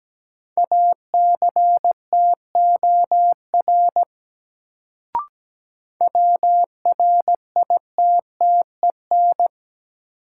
Morse code, heard as a keyed tone, about 17 wpm, 700 Hz. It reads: ACTOR WRITTEN
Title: Sentences from Top 1500 Words - New Word Drill Alt - 17wpm